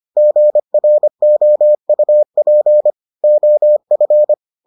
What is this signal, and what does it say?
Morse code, heard as a keyed tone, about 25 wpm, 600 Hz. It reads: GROUP OF